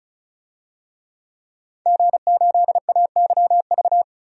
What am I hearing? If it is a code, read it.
Morse code, heard as a keyed tone, about 36 wpm, 700 Hz. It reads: G8AYV
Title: DXCC-100 Call Signs - 5 Characters - 35wpm